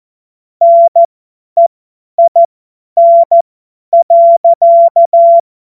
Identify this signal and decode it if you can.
Morse code, heard as a keyed tone, about 14 wpm, spaced as for 10 wpm, 680 Hz. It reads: NEIN.
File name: Morsekurs / Witze / Witz 01. November 2022 50 F70